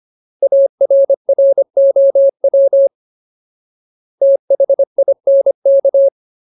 Morse code, heard as a keyed tone, about 25 wpm, 550 Hz. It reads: ARROW THINK
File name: Ultimate Word List / 25WPM Ultimate